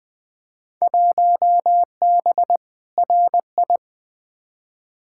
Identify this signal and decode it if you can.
Morse code, heard as a keyed tone, about 20 wpm, 700 Hz. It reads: 1B RI
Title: Field Day Exchanges Encore - 20wpm